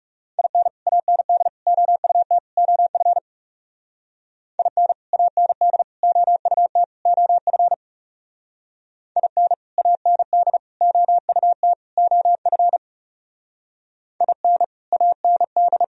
Morse code, heard as a keyed tone, about 37 wpm, 700 Hz. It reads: IN AND OUT OF IN AND OUT OF IN AND OUT OF IN AND
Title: Speed-Racing - Sets of 4 Words 30wpm